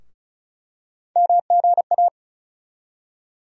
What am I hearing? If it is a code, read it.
Morse code, heard as a keyed tone, about 36 wpm, 700 Hz. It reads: MGA
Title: USA Call Sign Postfixes - 3 Letters 35wpm